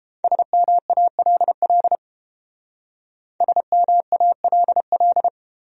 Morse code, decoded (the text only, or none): SMALL SMALL